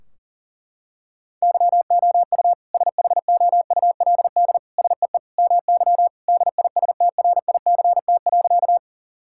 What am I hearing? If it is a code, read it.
Morse code, heard as a keyed tone, about 40 wpm, 700 Hz. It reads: YOU SHOULD SEE MY DISTRICT.
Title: Sentences from Top 1600 Words 40wpm